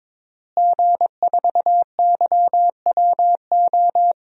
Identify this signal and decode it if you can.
Morse code, heard as a keyed tone, about 21 wpm, 700 Hz. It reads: G4YWO
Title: DXCC-100 Call Signs - 5 Characters - Repeated 3x - 22wpm